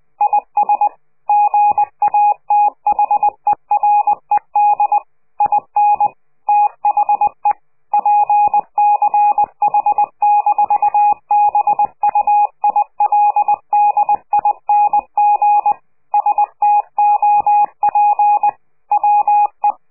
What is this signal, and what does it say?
Morse code, heard as a keyed tone, about 20 wpm, 830 Hz. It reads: IS GATHERED IN THE PCH-BUILDING STOP WE
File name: CQ de PCH laatste bericht farewell